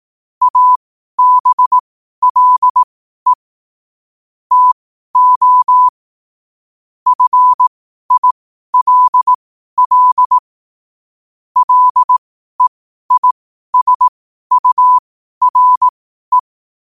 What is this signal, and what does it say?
Morse code, heard as a keyed tone, about 18 wpm, spaced as for 12 wpm, 1000 Hz. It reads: ABLE TO FILL LEISURE
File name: qotdfiles / 12wpm